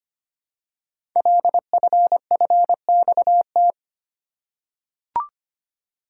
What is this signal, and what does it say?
Morse code, heard as a keyed tone, about 25 wpm, 700 Hz. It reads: LFFXT
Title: Code Groups - 5 Letters - 25wpm